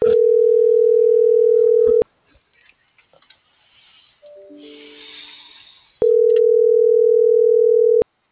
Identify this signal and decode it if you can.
Ringback tone (North America), 440 + 480 Hz.